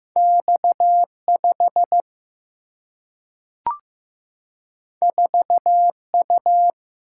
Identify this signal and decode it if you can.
Morse code, heard as a keyed tone, about 15 wpm, 700 Hz. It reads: X5 4U
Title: Single-Letter Single-Number 15wpm